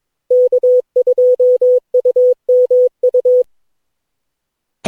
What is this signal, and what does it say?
Morse code, heard as a keyed tone, about 21 wpm, 500 Hz. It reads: K2UMU
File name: cw / Session9 cw training